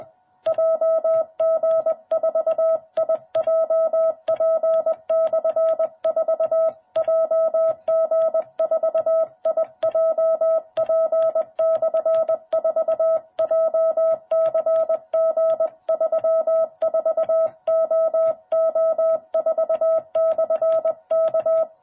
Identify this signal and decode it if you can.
Morse code, heard as a keyed tone, about 20 wpm, 660 Hz. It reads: JG4IJP/4 JG4IJP/4 JCG34OO4/K
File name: jh8cbh / 2021cw